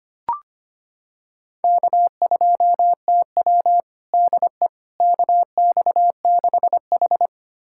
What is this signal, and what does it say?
Morse code, heard as a keyed tone, about 25 wpm, 700 Hz. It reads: K2TW DE KX6H
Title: USA Callsign DE Callsign - 4 Characters - 25wpm